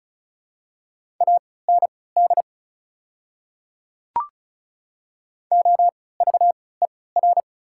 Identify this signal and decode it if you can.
Morse code, heard as a keyed tone, about 34 wpm, spaced as for 20 wpm, 700 Hz. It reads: AND OVER
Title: 3x Character Spacing - Top 300 Words - 35wpm